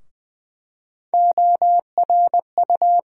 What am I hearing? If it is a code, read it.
Morse code, heard as a keyed tone, about 20 wpm, 700 Hz. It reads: ORU
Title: USA Call Sign Postfixes - 3 Letters 20wpm